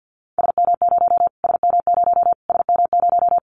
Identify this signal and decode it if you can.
Morse code, heard as a keyed tone, about 50 wpm, 700 Hz. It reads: SM0 SM0 SM0
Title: Callsign Prefixes for Top 25 DXCC - Repeated 3x - 50wpm